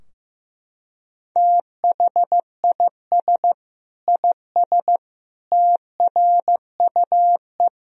Morse code, heard as a keyed tone, about 15 wpm, 700 Hz. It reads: THIS IS TRUE
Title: Sentences from Top 1000 Words - Review Part 3 - 15wpm